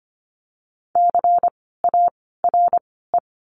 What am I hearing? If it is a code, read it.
Morse code, heard as a keyed tone, about 25 wpm, spaced as for 16 wpm, 700 Hz. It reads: CARE
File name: course / 25:10wpm / 094 Lesson 16f - Letter c - Words 25wpm